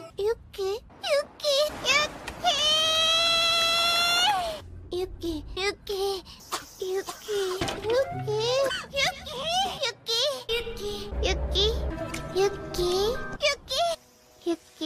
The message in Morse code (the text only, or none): none